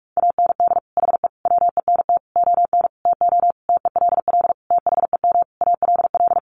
Morse code, heard as a keyed tone, about 44 wpm, 700 Hz. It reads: AND HE WENT ON TO TELL THEM ALL
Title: Sentences from Top 900 Words 45wpm